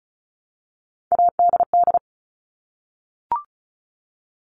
Morse code, heard as a keyed tone, about 36 wpm, 700 Hz. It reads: ADD
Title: Multi-letter Phonemes Repeated 3x 35wpm